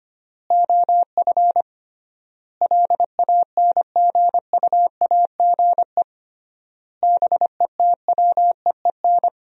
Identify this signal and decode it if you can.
Morse code, heard as a keyed tone, about 25 wpm, 700 Hz. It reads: OF LANGUAGE BETWEEN
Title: 3x Word-Spacing - Sentences from Top 500 Words - 25wpm